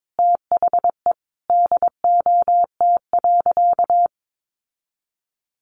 Morse code, heard as a keyed tone, about 22 wpm, 700 Hz. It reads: THE DOT.
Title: Sentences from Top 1500 Words 22wpm